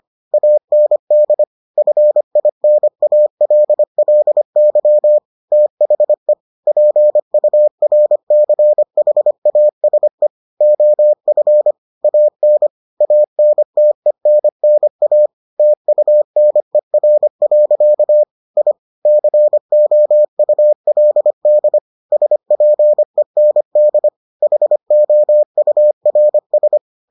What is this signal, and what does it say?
Morse code, heard as a keyed tone, about 25 wpm, 600 Hz. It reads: AND FINALLY THE PURCHASE OF AN ANTENNA TUNER. I COULD SPEND HOURS